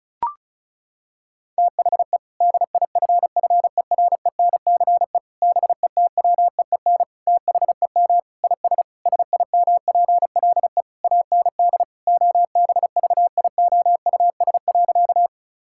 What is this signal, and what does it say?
Morse code, heard as a keyed tone, about 36 wpm, 700 Hz. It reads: THE DIFFERENCE BETWEEN THEM IS SIMPLE AND OBVIOUS.
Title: Sentences from Top 1600 Words 35wpm